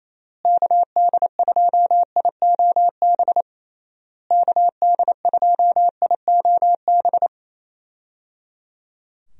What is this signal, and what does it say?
Morse code, heard as a keyed tone, about 28 wpm, 700 Hz. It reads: KD2IOB KD2IOB
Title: USA Call Signs - 6 Characters - Repeated 3x - 28wpm